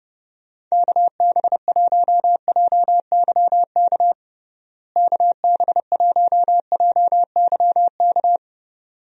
Morse code, heard as a keyed tone, about 32 wpm, 700 Hz. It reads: KB1JYK KB1JYK
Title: USA Call Signs - 6 Characters - Repeated 2x - 30wpm